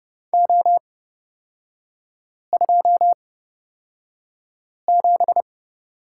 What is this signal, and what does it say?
Morse code, heard as a keyed tone, about 30 wpm, spaced as for 5 wpm, 700 Hz. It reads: O27
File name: Lesson 38e - Letter q - Three Characters 30wpm